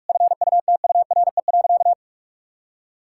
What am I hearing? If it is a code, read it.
Morse code, heard as a keyed tone, about 46 wpm, 700 Hz. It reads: FUTURE.
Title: Sentences from Top 1000 Words - Review Part 6 - 45wpm